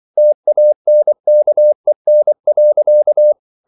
Morse code, heard as a keyed tone, about 24 wpm, 600 Hz. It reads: TANKEN.
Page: Ferie 24 wpm